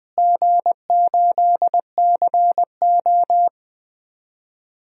Morse code, heard as a keyed tone, about 20 wpm, 700 Hz. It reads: G8CO
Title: DXCC-100 Call Signs - 4 Characters - Repeated 2x - 20wpm